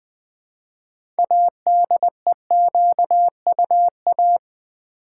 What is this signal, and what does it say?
Morse code, heard as a keyed tone, about 20 wpm, 700 Hz. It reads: ADEQUA
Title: Build A Word - 3 Letters At A Time - 8-Letter Words - 20wpm